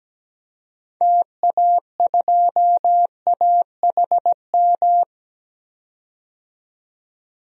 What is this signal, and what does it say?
Morse code, heard as a keyed tone, about 17 wpm, 700 Hz. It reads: TA2AHM
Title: DXCC-100 Call Signs - 6 Characters - 17wpm